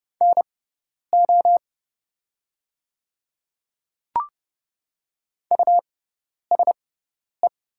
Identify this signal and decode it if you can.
Morse code, heard as a keyed tone, about 30 wpm, spaced as for 10 wpm, 700 Hz. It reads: NO USE
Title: 6x Character Spacing - Top 100 Words - 30wpm